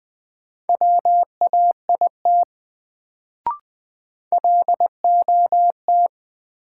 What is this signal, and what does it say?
Morse code, heard as a keyed tone, about 21 wpm, 700 Hz. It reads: WAIT LOT
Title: Sentences from Top 500 Words - New Word Drill - 20wpm